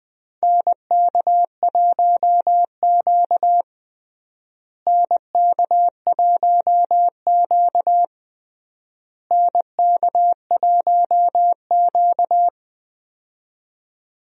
Morse code, heard as a keyed tone, about 20 wpm, 700 Hz. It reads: NK1Q NK1Q NK1Q
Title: USA Call Signs - 4 Characters - Repeated 3x - 20wpm